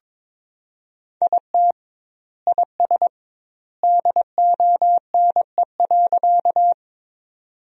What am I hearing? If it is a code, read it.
Morse code, heard as a keyed tone, about 22 wpm, 700 Hz. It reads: IT IS DONE.
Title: 2x Word-Spacing - Sentences from Top 500 Words - 22wpm